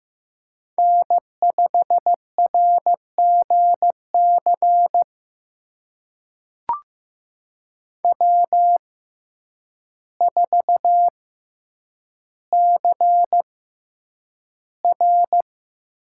Morse code, heard as a keyed tone, about 15 wpm, spaced as for 8 wpm, 700 Hz. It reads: N5RGC W4CR
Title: USA Call Signs - 6x Character Spacing - 15wpm